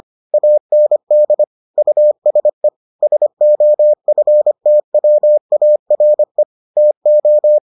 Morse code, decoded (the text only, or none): AND USE SOFTWARE TO